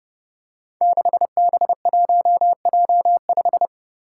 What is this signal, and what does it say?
Morse code, heard as a keyed tone, about 30 wpm, 700 Hz. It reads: 6B1J5
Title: Five Challenging Letter-Number Combinations 30wpm